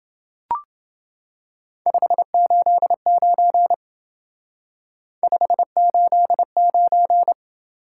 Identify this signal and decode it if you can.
Morse code, heard as a keyed tone, about 30 wpm, 700 Hz. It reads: 589 589